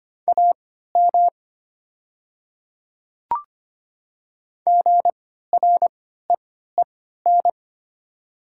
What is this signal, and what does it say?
Morse code, heard as a keyed tone, about 25 wpm, spaced as for 14 wpm, 690 Hz. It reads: AM GREEN
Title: 3x Character Spacing - Top 500 Words - 25wpm